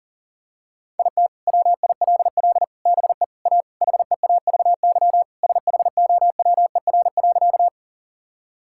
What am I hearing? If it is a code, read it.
Morse code, heard as a keyed tone, about 40 wpm, 700 Hz. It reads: IT WILL BE A HEAVY SHOWER.
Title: Sentences from Top 1600 Words 40wpm